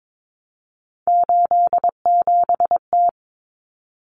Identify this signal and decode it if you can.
Morse code, heard as a keyed tone, about 22 wpm, 700 Hz. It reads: 87T
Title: Common Cut Numbers - 3-Digit 22wpm